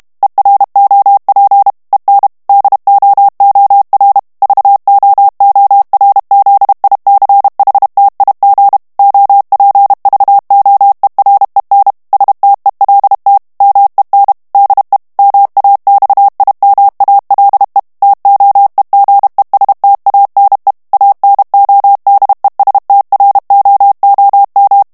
Morse code, heard as a keyed tone, about 32 wpm, 800 Hz. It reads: EROP EN DOOR VOORZICHTIG OPVOEREN STELT MEN DE MAXIMALE TOEGESTANE ANODESTROOM
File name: HOLLANDAIS texte clair 2 vitesse 30 mots minute